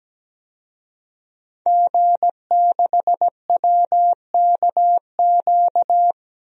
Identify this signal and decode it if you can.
Morse code, heard as a keyed tone, about 17 wpm, 700 Hz. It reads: G6WKQ
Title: DXCC-100 Call Signs - 5 Characters - 17wpm